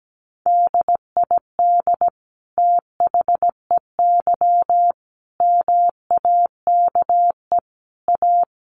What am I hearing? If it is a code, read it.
Morse code, heard as a keyed tone, about 17 wpm, 700 Hz. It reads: DID THEY MAKE A
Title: Sentences from Top 800 Words 17wpm